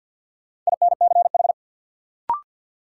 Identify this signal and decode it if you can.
Morse code, heard as a keyed tone, about 47 wpm, 700 Hz. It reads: INCH
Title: Sentences from Top 1000 Words - Review Part 3 - New Word Drill - 50wpm